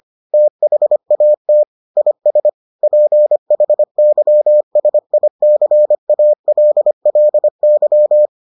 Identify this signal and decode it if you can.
Morse code, heard as a keyed tone, about 25 wpm, 600 Hz. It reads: THAT IS PHYSICALLY